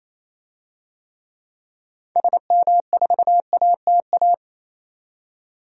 Morse code, heard as a keyed tone, about 28 wpm, 700 Hz. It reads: SM4ATA
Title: DXCC-100 Call Signs - 6 Characters - 28wpm